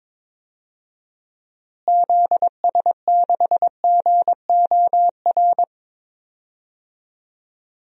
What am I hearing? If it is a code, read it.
Morse code, heard as a keyed tone, about 22 wpm, 700 Hz. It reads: ZS6GOR
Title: DXCC-100 Call Signs - 6 Characters - Repeated 3x - 22wpm